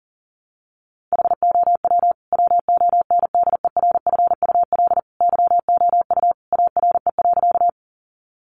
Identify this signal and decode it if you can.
Morse code, heard as a keyed tone, about 37 wpm, 700 Hz. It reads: HOW WONDERFUL YOU ARE.
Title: Sentences from Top 1100 Words 40wpm